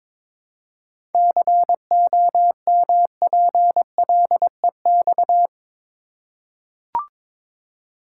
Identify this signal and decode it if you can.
Morse code, heard as a keyed tone, about 22 wpm, 700 Hz. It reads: COMPLEX
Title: Seven Letter Words - 22wpm